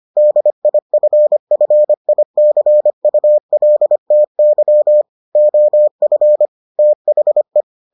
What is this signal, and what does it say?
Morse code, heard as a keyed tone, about 25 wpm, 600 Hz. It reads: DIFFICULTY OF THE